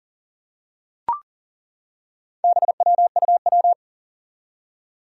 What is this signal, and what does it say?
Morse code, heard as a keyed tone, about 40 wpm, 700 Hz. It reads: BWUW
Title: Four Challenging Letter-Number Combinations 40wpm